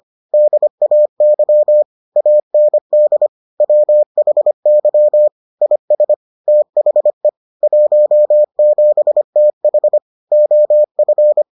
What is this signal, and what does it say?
Morse code, heard as a keyed tone, about 25 wpm, 600 Hz. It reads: DAY AND WHY IS THE 17TH OF